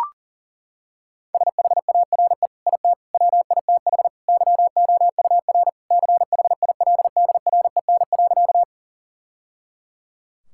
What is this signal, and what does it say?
Morse code, heard as a keyed tone, about 40 wpm, 700 Hz. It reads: SHARE IT WITH YOUR CHILDREN.
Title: Sentences from Top 1000 Words - Review Part 6 - 40wpm